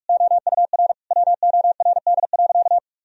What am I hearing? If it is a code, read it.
Morse code, heard as a keyed tone, about 41 wpm, 700 Hz. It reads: OUR WORD.
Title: Sentences from Top 1000 Words - Review Part 3 - 45wpm